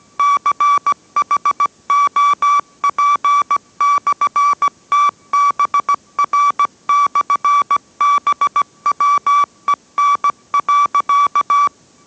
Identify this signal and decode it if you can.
Morse code, heard as a keyed tone, about 17 wpm, 1150 Hz. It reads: CHOP/TBR/BWEN.